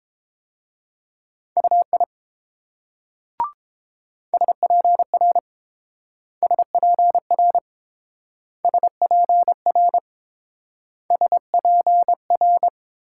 Morse code, heard as a keyed tone, about 28 wpm, 700 Hz. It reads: UI SPR SPR SPR SPR